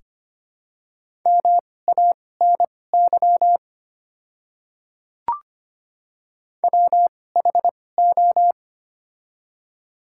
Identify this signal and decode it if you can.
Morse code, heard as a keyed tone, about 25 wpm, spaced as for 18 wpm, 700 Hz. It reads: MANY WHO
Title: 2x Character Spacing - Top 100 Words - 25wpm